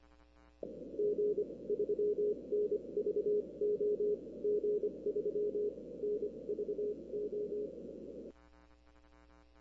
Morse code, heard as a keyed tone, about 24 wpm, spaced as for 20 wpm, 420 Hz. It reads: G3NVO G3NVO
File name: arrl cw dx